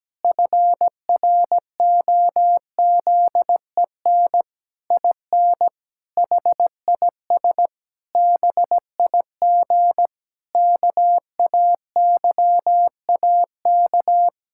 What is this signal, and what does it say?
Morse code, heard as a keyed tone, about 17 wpm, 700 Hz. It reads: FROZEN IN HIS BIG KAYAK
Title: Pangram Challenge v2 - 17wpm